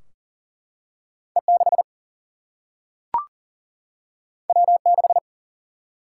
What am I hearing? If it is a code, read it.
Morse code, heard as a keyed tone, about 38 wpm, 700 Hz. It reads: E 6 W 6
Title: Single-Letter Single-Number 40wpm